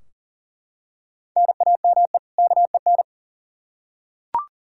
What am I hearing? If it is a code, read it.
Morse code, heard as a keyed tone, about 40 wpm, 700 Hz. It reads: NAME KEN
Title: Practice QSO - Exchange 1 - Beginning - 40wpm